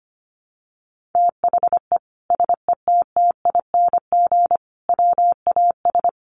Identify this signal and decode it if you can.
Morse code, heard as a keyed tone, about 25 wpm, 700 Hz. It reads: THE SETTING WAS
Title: Sentences from Top 1300 Words 25wpm